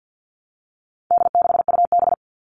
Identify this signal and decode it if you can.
Morse code, heard as a keyed tone, about 50 wpm, 700 Hz. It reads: D6UB
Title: All Cut Numbers - 4-Digit 50wpm